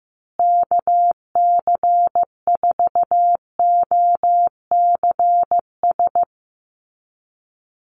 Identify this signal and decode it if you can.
Morse code, heard as a keyed tone, about 15 wpm, 700 Hz. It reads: KC4OCS
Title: USA Call Signs Repeated 2x 15wpm